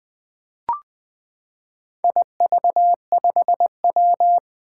Morse code, heard as a keyed tone, about 20 wpm, 700 Hz. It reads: IV5W
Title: Four Challenging Letter-Number Combinations 20wpm